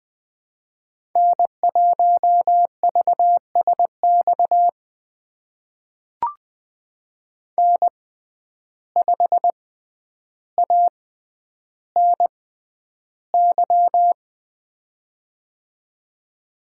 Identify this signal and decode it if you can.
Morse code, heard as a keyed tone, about 20 wpm, spaced as for 10 wpm, 700 Hz. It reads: N1VSX N5ANY
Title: USA Call Signs - 5 Characters - 6x Character Spacing - 20wpm